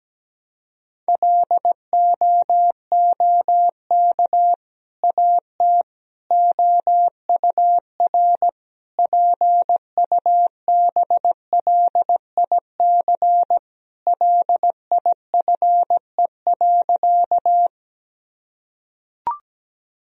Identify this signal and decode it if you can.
Morse code, heard as a keyed tone, about 17 wpm, 700 Hz. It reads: LOOK AT OUR PUBLIC LIFE.
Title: Sentences from Top 1000 Words - Review Part 3 - 17wpm